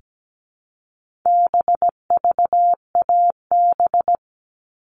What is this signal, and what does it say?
Morse code, heard as a keyed tone, about 17 wpm, 700 Hz. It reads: BVAB
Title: All Cut Numbers - 4-Digit 17wpm